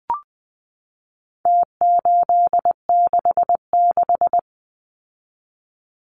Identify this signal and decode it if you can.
Morse code, heard as a keyed tone, about 20 wpm, 700 Hz. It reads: T866
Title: Numbers - Contest Style - 4-Digit 20wpm